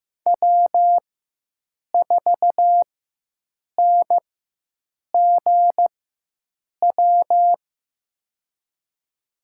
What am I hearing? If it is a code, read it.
Morse code, heard as a keyed tone, about 15 wpm, spaced as for 7 wpm, 700 Hz. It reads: W4NGW
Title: USA Call Signs - 5 Characters - 4x Character Spacing - 15wpm